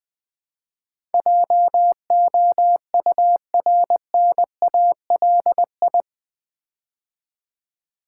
Morse code, heard as a keyed tone, about 20 wpm, 700 Hz. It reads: JOURNALI
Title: Build A Word - 4 Letters At A Time - 10-Letter Words - 20wpm